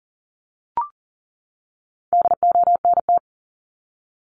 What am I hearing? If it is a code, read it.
Morse code, heard as a keyed tone, about 40 wpm, 700 Hz. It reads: DONT
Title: Top 300 Words 40wpm